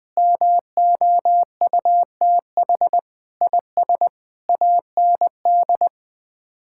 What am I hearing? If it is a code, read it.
Morse code, heard as a keyed tone, about 20 wpm, 700 Hz. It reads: MOUTH IS AND
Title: Sets of 8 Words - 20wpm